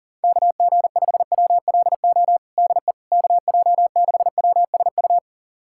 Morse code, heard as a keyed tone, about 40 wpm, 700 Hz. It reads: KG5WLO DE KJ6WSU
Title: USA Callsign DE Callsign 40wpm